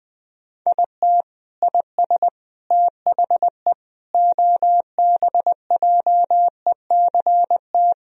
Morse code, heard as a keyed tone, about 21 wpm, 700 Hz. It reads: IT IS THE OBJECT
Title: Sentences from Top 1000 Words - Review Part 4 - 20wpm